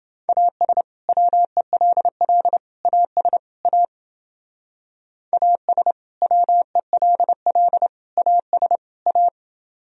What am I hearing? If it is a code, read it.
Morse code, heard as a keyed tone, about 28 wpm, 700 Hz. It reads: AS WELL AS A AS WELL AS A